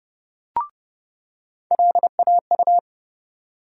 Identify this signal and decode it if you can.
Morse code, heard as a keyed tone, about 30 wpm, 700 Hz. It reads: LAU
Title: USA Call Sign Postfixes - 3 Letters 30wpm